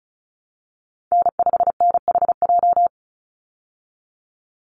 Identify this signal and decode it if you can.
Morse code, heard as a keyed tone, about 34 wpm, 700 Hz. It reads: N5NHJ
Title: Call Signs Repeated 3x - 35wpm